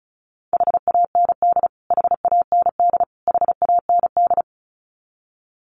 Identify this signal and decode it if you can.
Morse code, heard as a keyed tone, about 34 wpm, 700 Hz. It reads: HAND HAND HAND
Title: Top 500 Words Repeated 35wpm